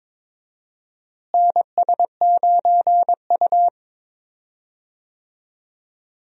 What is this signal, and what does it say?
Morse code, heard as a keyed tone, about 22 wpm, 700 Hz. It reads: NS9U
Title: USA Call Signs - 4 Characters - Repeated 2x - 22wpm